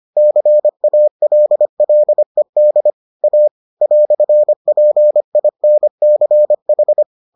Morse code, heard as a keyed tone, about 25 wpm, 600 Hz. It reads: CALLED A "PINCH